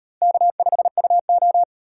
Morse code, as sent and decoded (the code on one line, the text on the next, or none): -.- ..... ..- ---
K5UO